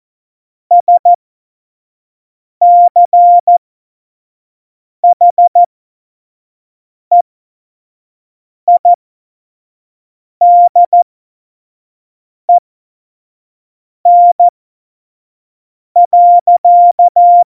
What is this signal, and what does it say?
Morse code, heard as a keyed tone, about 14 wpm, spaced as for 5 wpm, 700 Hz. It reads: SCHEIDEN.
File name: Morsekurs / Witze / Witz 06. November 2018, 25 F70